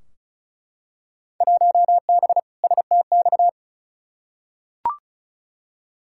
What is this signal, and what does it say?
Morse code, heard as a keyed tone, about 34 wpm, 700 Hz. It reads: 1B STX
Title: Field Day Exchanges - 35wpm